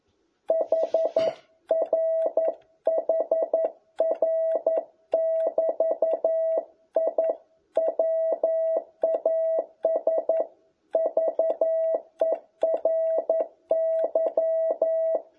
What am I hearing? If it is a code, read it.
Morse code, heard as a keyed tone, about 11 wpm, 650 Hz. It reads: HR HR<BT>I WAS VERY
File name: jh8cbh / 2021cw